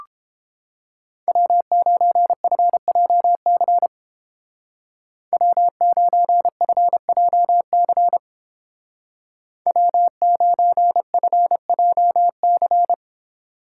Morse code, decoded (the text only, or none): W9FJC W9FJC W9FJC